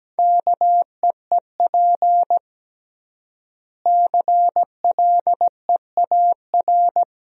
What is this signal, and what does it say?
Morse code, heard as a keyed tone, about 17 wpm, 700 Hz. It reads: KEEP CLEAR
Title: 3x Word-Spacing - Sentences from Top 500 Words - 17wpm